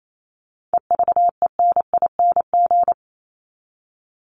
Morse code, heard as a keyed tone, about 28 wpm, 700 Hz. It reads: EVENING
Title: Top 1501-1750 Words 28wpm